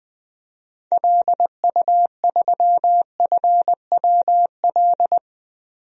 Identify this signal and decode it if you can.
Morse code, heard as a keyed tone, about 20 wpm, 700 Hz. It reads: LU3FWL
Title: DXCC-100 Call Signs - 6 Characters - 20wpm